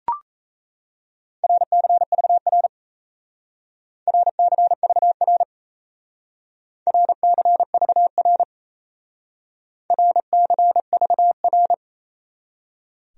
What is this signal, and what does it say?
Morse code, heard as a keyed tone, about 34 wpm, 700 Hz. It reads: RCVR RCVR RCVR RCVR
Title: Speed-Racing Abbreviations - 28wpm